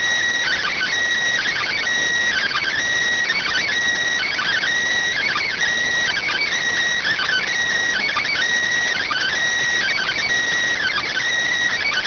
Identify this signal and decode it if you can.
Emergency Alert System (SAME) tones (header bursts), heard as harsh buzzing bursts. It is heard from 0.4 to 11.3 s.